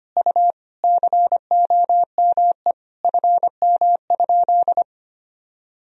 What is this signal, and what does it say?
Morse code, heard as a keyed tone, about 25 wpm, 700 Hz. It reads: U COME FM?
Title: Sentences from Top 200 Words and Abbreviations 25wpm